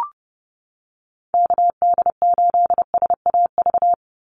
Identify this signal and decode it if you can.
Morse code, heard as a keyed tone, about 28 wpm, 700 Hz. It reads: KD8SAV
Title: Call Signs - 30wpm